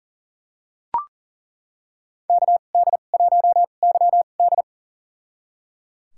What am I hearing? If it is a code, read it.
Morse code, heard as a keyed tone, about 37 wpm, spaced as for 28 wpm, 690 Hz. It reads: KD1YD
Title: USA Call Signs - 2x Character Spacing - 40wpm